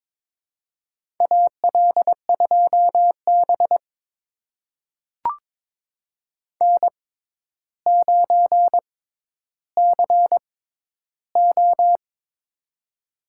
Morse code, heard as a keyed tone, about 22 wpm, spaced as for 11 wpm, 700 Hz. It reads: AL2B N9CO
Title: USA Call Signs - 4 Characters - 6x Character Spacing - 22wpm